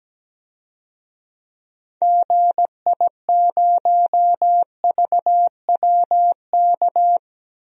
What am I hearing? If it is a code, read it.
Morse code, heard as a keyed tone, about 17 wpm, 700 Hz. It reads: GI0VWK
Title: DXCC-100 Call Signs - 6 Characters - 17wpm